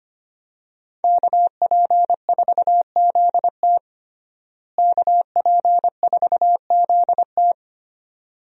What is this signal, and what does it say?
Morse code, heard as a keyed tone, about 25 wpm, 700 Hz. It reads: KP4ZT KP4ZT
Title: USA Call Signs - 5 Characters - Repeated 2x - 25wpm